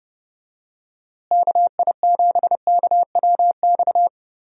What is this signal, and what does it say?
Morse code, heard as a keyed tone, about 30 wpm, 700 Hz. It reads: KI7KWX
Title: USA Call Signs - 6 Characters - Repeated 3x - 30wpm